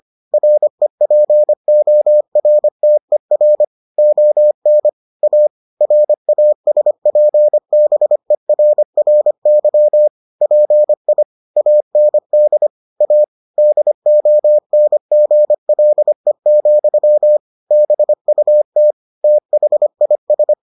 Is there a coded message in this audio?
Yes: Morse code from 0.3 to 20.5 s, heard as a keyed tone, about 25 wpm, 600 Hz. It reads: REPORTER ON A RASPBERRY PI AND A DONGLE, BUT THIS